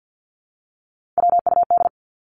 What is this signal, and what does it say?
Morse code, heard as a keyed tone, about 49 wpm, 700 Hz. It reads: WUD
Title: Abbreviations 50wpm